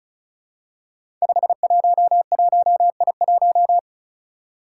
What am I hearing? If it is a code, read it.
Morse code, heard as a keyed tone, about 36 wpm, 700 Hz. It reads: 511I1
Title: Five Challenging Letter-Number Combinations 35wpm